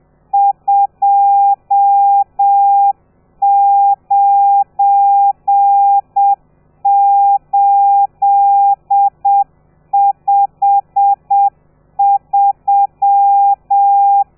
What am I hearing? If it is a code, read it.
Morse code, heard as a keyed tone, about 7 wpm, 800 Hz. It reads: 29853
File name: telegrafia / i2tfj / Lez12 30